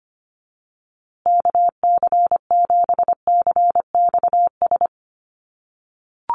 Morse code, heard as a keyed tone, about 25 wpm, 700 Hz. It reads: KC7CXS
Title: USA Call Signs Repeated 3x 25wpm